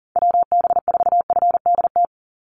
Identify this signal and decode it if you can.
Morse code, heard as a keyed tone, about 39 wpm, 700 Hz. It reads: WB4FDT
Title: Call Signs Repeated 3x - 40wpm